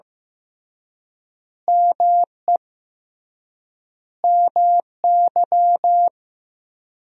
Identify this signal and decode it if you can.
Morse code, heard as a keyed tone, about 15 wpm, 700 Hz. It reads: ME MY
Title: 3x Word-Spacing - Sentences from Top 500 Words - 15wpm